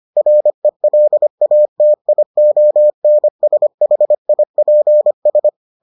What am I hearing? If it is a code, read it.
Morse code, heard as a keyed tone, about 25 wpm, 600 Hz. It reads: RELATIONSHIPS